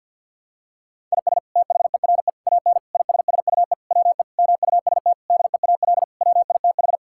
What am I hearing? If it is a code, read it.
Morse code, heard as a keyed tone, about 50 wpm, 700 Hz. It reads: IS THERE AN ISSUE WE MUST DEAL WITH
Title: Sentences from Top 1000 Words - Review Part 6 - 50wpm